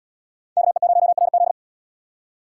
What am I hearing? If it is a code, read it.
Morse code, heard as a keyed tone, about 69 wpm, 700 Hz. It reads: KE0RZ